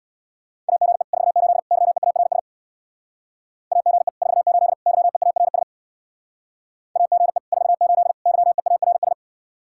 Morse code, heard as a keyed tone, about 61 wpm, 700 Hz. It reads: AGE 48 YEARS AGE 48 YEARS AGE 48 YEARS
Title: Speed-Racing Common QSO Phrases - 50wpm